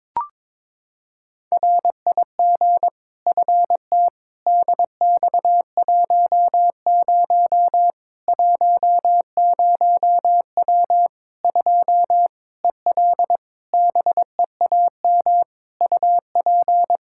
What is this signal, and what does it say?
Morse code, heard as a keyed tone, about 22 wpm, 700 Hz. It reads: RIG FT DX10 10W 2 EL BEAM UP
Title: Practice QSO - Exchange 2 - Info - 22wpm